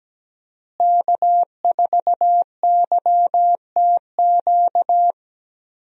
Morse code, heard as a keyed tone, about 17 wpm, 700 Hz. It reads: K4YTQ
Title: USA Call Signs - 5 Characters - Repeated 3x - 17wpm